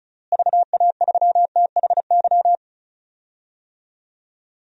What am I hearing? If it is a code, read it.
Morse code, heard as a keyed tone, about 34 wpm, 700 Hz. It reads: VA3THY
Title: DXCC-100 Call Signs - 6 Characters - Repeated 3x - 35wpm